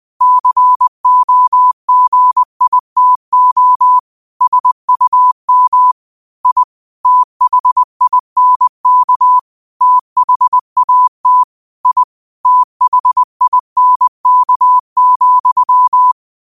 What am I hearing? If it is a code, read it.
Morse code, heard as a keyed tone, about 20 wpm, 1000 Hz. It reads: COGITO SUM I THINK THAT I THINK,